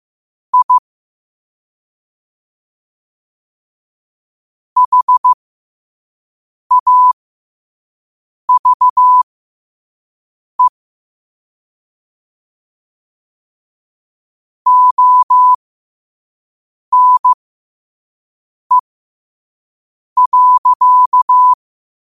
Morse code, heard as a keyed tone, about 15 wpm, spaced as for 5 wpm, 1000 Hz. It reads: I HAVE ONE.
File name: qotdfiles / 5wpm